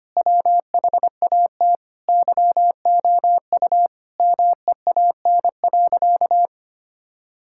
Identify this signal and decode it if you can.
Morse code, heard as a keyed tone, about 25 wpm, 700 Hz. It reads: WHAT YOU MEAN.
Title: Sentences from Top 1000 Words - Review Part 6 - 25wpm